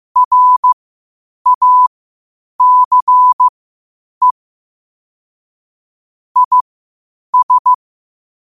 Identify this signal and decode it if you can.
Morse code, heard as a keyed tone, about 15 wpm, spaced as for 8 wpm, 1000 Hz. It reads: RACE IS
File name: qotdfiles / 8wpm